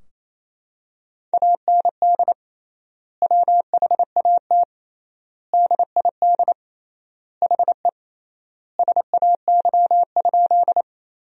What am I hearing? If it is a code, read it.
Morse code, heard as a keyed tone, about 28 wpm, 700 Hz. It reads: AND WHAT DID HE SAY?
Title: 3x Word-Spacing - Sentences from Top 200 Words - 28wpm